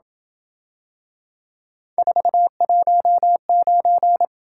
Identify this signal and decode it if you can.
Morse code, heard as a keyed tone, about 27 wpm, 700 Hz. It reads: 419